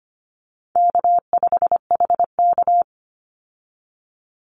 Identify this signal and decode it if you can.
Morse code, heard as a keyed tone, about 25 wpm, 700 Hz. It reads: K5HK
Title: Call Signs Repeated 2x - 25wpm